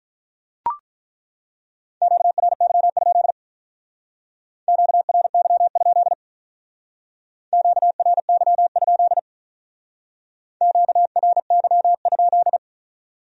Morse code, heard as a keyed tone, about 43 wpm, 700 Hz. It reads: QRY? QRY? QRY? QRY?
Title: Speed-Racing Q-Codes - 35wpm